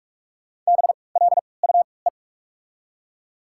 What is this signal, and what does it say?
Morse code, heard as a keyed tone, about 48 wpm, spaced as for 26 wpm, 690 Hz. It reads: BLUE
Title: 3x Character Spacing - Top 500 Words - 45wpm